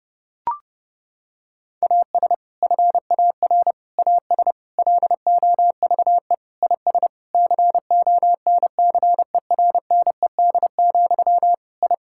Morse code, heard as a keyed tone, about 30 wpm, 700 Hz. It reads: AS FAR AS LOVE IS CONCERNED, I
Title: Sentences from Top 1000 Words - Review Part 6 - 30wpm